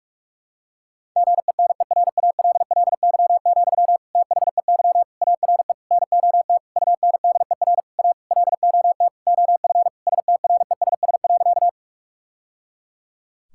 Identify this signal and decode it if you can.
Morse code, heard as a keyed tone, about 46 wpm, 700 Hz. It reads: GENERALLY, THEY ARE NOT UNDER A LOT OF STRESS.
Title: Sentences from Top 1000 Words - Review Part 6 - 45wpm